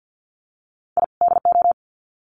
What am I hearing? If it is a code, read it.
Morse code, heard as a keyed tone, about 50 wpm, 700 Hz. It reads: I DO
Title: Sets of 2 Words 50wpm